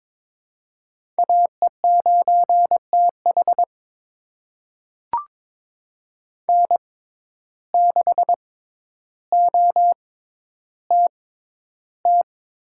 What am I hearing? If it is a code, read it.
Morse code, heard as a keyed tone, about 22 wpm, spaced as for 11 wpm, 700 Hz. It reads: AE9TH N6OTT
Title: USA Call Signs - 5 Characters - 6x Character Spacing - 22wpm